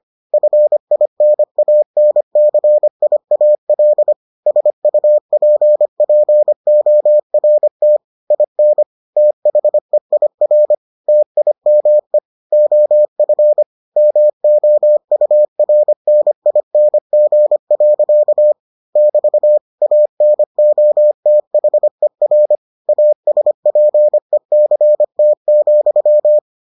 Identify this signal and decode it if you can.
Morse code, heard as a keyed tone, about 25 wpm, 600 Hz. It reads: FINANCIAL SUPPORT IN THEIR TIME OF MOURNING. <BT> ANOTHER ASPECT,